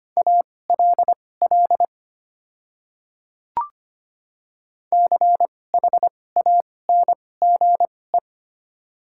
Morse code, heard as a keyed tone, about 25 wpm, spaced as for 18 wpm, 700 Hz. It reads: ALL CHANGE
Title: 2x Character Spacing - Top 200 Words - 25wpm